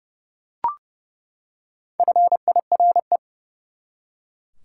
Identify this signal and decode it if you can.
Morse code, heard as a keyed tone, about 30 wpm, 700 Hz. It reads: FIRE
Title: Four Letter Words - 30wpm